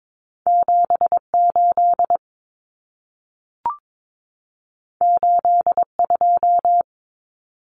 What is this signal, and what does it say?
Morse code, heard as a keyed tone, about 22 wpm, 700 Hz. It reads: 78 82
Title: Numbers - 2-Digit 22wpm